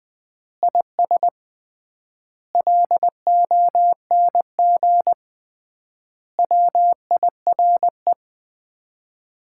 Morse code, heard as a keyed tone, about 21 wpm, 700 Hz. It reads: IS LONG WIRE
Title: Common QSO Phrases - 3x Word-Spacing - 20wpm